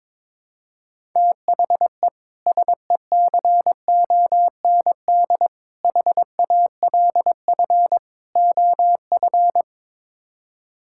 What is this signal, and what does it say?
Morse code, heard as a keyed tone, about 22 wpm, 700 Hz. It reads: THE SECOND HALF OF
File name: practice / 22wpm / Build A Phrase - 5-Word Phrases - 22wpm